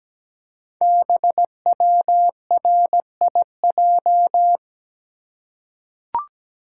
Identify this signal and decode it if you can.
Morse code, heard as a keyed tone, about 17 wpm, 700 Hz. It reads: BWRIJ
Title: Code Groups - 5 Letters - 17wpm